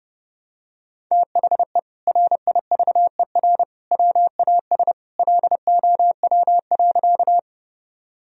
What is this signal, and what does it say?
Morse code, heard as a keyed tone, about 30 wpm, 700 Hz. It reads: THE RIVER WAS LOW.
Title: Sentences from Top 1000 Words - Review Part 1 - 30wpm